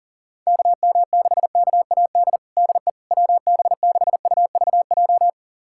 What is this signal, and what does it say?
Morse code, heard as a keyed tone, about 40 wpm, 700 Hz. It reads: KM6KAD DE WB6UVJ
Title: USA Callsign DE Callsign - 6 Characters - 40wpm